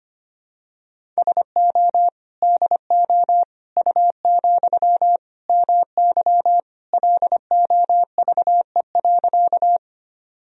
Morse code, heard as a keyed tone, about 25 wpm, 700 Hz. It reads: SO DO U, MY LOVE.
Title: Sentences from Top 400 Words and Abbreviations 25wpm